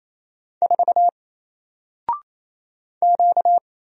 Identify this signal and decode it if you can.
Morse code, heard as a keyed tone, about 27 wpm, 700 Hz. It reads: 4 Q
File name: Single-Letter-Number-28wpm